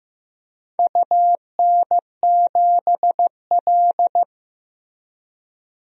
Morse code, heard as a keyed tone, about 15 wpm, 700 Hz. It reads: UN7L
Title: DXCC-100 Call Signs - 4 Characters - Repeated 2x - 15wpm